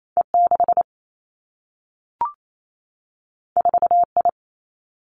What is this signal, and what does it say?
Morse code, heard as a keyed tone, about 28 wpm, 700 Hz. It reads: E 6 4 I
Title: Two Challenging Letter-Number Combinations 28wpm